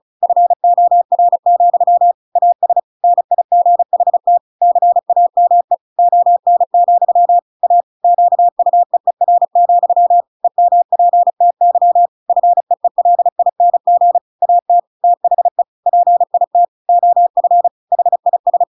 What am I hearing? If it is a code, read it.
Morse code, heard as a keyed tone, about 34 wpm, 700 Hz. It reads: FOR, AS NIGHT CAME ON, A QUEER, EMPTY FEELING AT THE PIT OF HIS
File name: Pinocchio / Pinocchio ch5 35 35 ik1hge